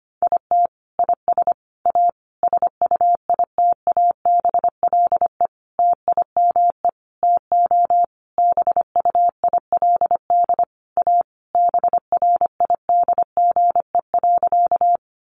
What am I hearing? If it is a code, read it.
Morse code, heard as a keyed tone, about 25 wpm, 700 Hz. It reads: IT IS A SUITABLE TIME TO BUILD A BRIDGE.
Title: Sentences from Top 1400 Words 25wpm